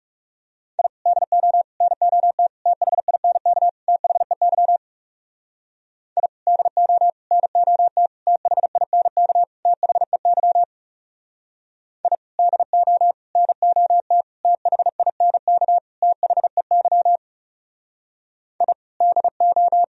Morse code, heard as a keyed tone, about 37 wpm, 700 Hz. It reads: I DO NOT THINK THEY I DO NOT THINK THEY I DO NOT THINK THEY I DO
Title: Speed-Racing - Sets of 5 Words 30wpm